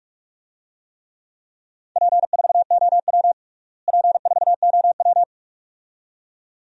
Morse code, heard as a keyed tone, about 42 wpm, 700 Hz. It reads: P4OW P4OW
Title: DXCC-100 Call Signs - 4 Characters - Repeated 2x - 45wpm